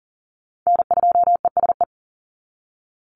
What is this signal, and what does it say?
Morse code, heard as a keyed tone, about 40 wpm, 700 Hz. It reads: N2ESE
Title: NA SOTA Chaser Callsigns 40wpm